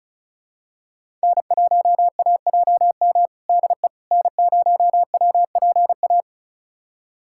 Morse code, heard as a keyed tone, about 36 wpm, 700 Hz. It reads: N1AJM DE N0WPA
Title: USA Callsign DE Callsign 35wpm